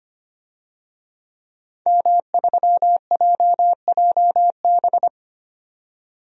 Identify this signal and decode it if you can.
Morse code, heard as a keyed tone, about 25 wpm, 690 Hz. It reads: M3JJB
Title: DXCC-100 Call Signs - 5 Characters - 25wpm